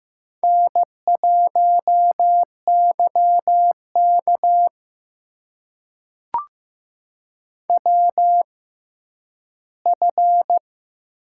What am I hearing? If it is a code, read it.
Morse code, heard as a keyed tone, about 15 wpm, spaced as for 10 wpm, 700 Hz. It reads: N1YK WF
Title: USA Call Signs - 4 Characters - 6x Character Spacing - 15wpm